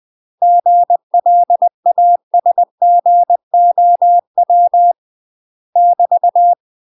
Morse code, heard as a keyed tone, about 20 wpm, 700 Hz. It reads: GLASGOW <BT>
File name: UK MASTER QSOs TRAINING / 20WPM / UK MASTER QSOs 0010